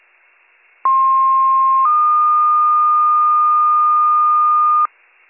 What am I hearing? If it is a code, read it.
Two-tone page, heard as a one-second tone, then a three-second tone, about 1060 then 1220 Hz, with noise. It is heard from 0.8 to 4.9 s.